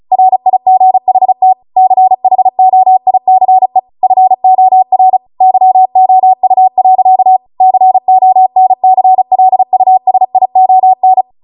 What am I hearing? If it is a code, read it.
Morse code, heard as a keyed tone, about 36 wpm, 750 Hz. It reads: RIGHT CHOICE FOR YOU. CONCLUSION